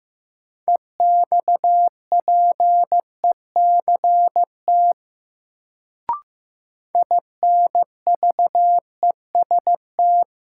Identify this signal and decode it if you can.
Morse code, heard as a keyed tone, about 15 wpm, 700 Hz. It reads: EXPECT INVEST
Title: Sentences from Top 1100 Words - New Word Drill - 15wpm